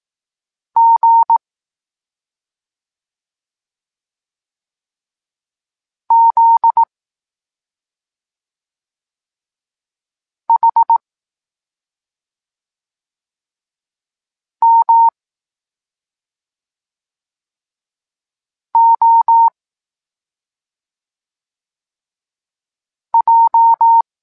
Morse code, heard as a keyed tone, about 18 wpm, 920 Hz. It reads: G Z H M O J
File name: Tekst-26 1